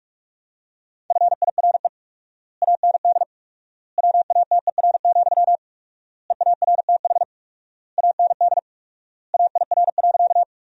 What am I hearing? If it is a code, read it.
Morse code, heard as a keyed tone, about 41 wpm, 700 Hz. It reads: FIRE AND WATER, EARTH AND AIR.
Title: 4x Word-Spacing - Sentences from Top 400 Words - 45wpm